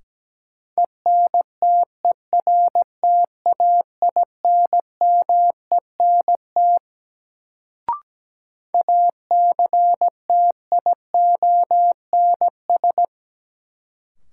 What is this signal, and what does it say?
Morse code, heard as a keyed tone, about 17 wpm, 700 Hz. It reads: ENTERTAINMENT ACTIONS
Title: Sentences from Top 1000 Words - Review Part 6 - New Word Drill - 17wpm